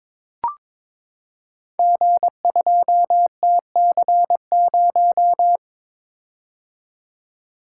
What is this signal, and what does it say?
Morse code, heard as a keyed tone, about 22 wpm, 690 Hz. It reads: G2TC0
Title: Code Groups - 5 Letter:Numbers - 22wpm